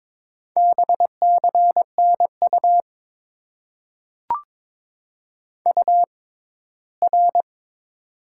Morse code, heard as a keyed tone, about 22 wpm, spaced as for 15 wpm, 700 Hz. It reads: BCNU UR
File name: Top 90 QSO Elements - 6x Character Spacing - 22wpm